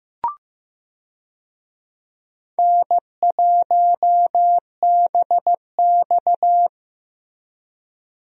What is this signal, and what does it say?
Morse code, heard as a keyed tone, about 15 wpm, 700 Hz. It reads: N1BX
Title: USA Call Signs - 4 Characters - Repeated 3x - 15wpm